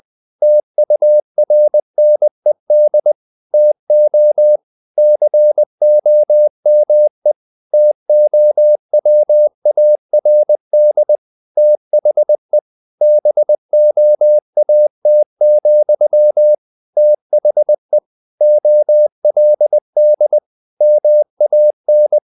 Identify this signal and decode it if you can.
Morse code, heard as a keyed tone, about 20 wpm, 600 Hz. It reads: TURNED TO COME TOWARD THE BOAT, THE OLD MAN
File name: the old man and the sea part 2 20wpm